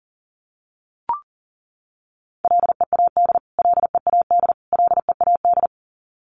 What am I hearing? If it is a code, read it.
Morse code, heard as a keyed tone, about 40 wpm, 700 Hz. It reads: LEAD LEAD LEAD
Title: Top 500 Words Repeated 40wpm